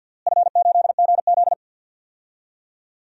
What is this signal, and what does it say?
Morse code, heard as a keyed tone, about 52 wpm, 700 Hz. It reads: F8GZ
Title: DXCC-100 Call Signs - 4 Characters - Repeated 2x - 50wpm